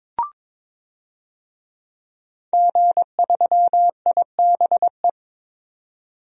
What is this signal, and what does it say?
Morse code, heard as a keyed tone, about 22 wpm, 700 Hz. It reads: G3IBE
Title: DXCC-100 Call Signs - 5 Characters - 22wpm